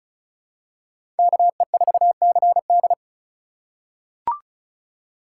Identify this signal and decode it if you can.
Morse code, heard as a keyed tone, about 36 wpm, 700 Hz. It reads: KE4CD
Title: USA Call Signs - 5 Characters - 4x Character Spacing - 35wpm